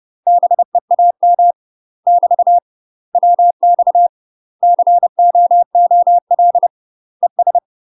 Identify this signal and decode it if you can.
Morse code, heard as a keyed tone, about 30 wpm, 700 Hz. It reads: BEAM <BT> WX COOL ES
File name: UK MASTER QSOs TRAINING / 30WPM / UK MASTER QSOs 0040